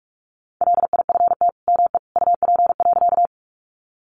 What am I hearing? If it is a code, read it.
Morse code, heard as a keyed tone, about 44 wpm, 700 Hz. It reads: LIFT ME UP.
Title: Sentences from Top 1000 Words 45wpm